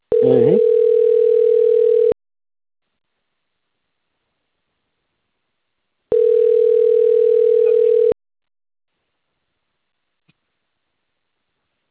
Ringback tone (North America), 440 + 480 Hz.
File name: out- - - - - .36407